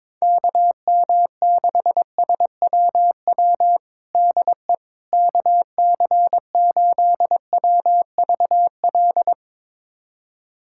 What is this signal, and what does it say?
Morse code, heard as a keyed tone, about 22 wpm, 700 Hz. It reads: KM6SWW DE KC8WVL
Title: USA Callsign DE Callsign 22wpm